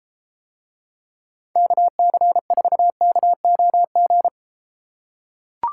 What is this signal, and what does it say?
Morse code, heard as a keyed tone, about 34 wpm, 700 Hz. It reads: KC4KOG